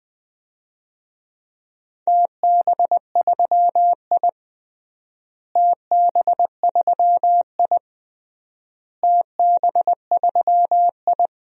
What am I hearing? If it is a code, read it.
Morse code, heard as a keyed tone, about 21 wpm, 700 Hz. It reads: TB3I TB3I TB3I
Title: DXCC-100 Call Signs - 4 Characters - Repeated 3x - 20wpm